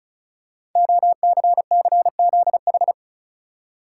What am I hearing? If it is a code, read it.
Morse code, heard as a keyed tone, about 36 wpm, 690 Hz. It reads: OCCZH
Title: Code Groups - 5 Letters - 35wpm